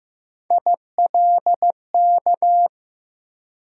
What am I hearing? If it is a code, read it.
Morse code, heard as a keyed tone, about 15 wpm, 700 Hz. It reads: ILK
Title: USA Call Sign Postfixes - 3 Letters 15wpm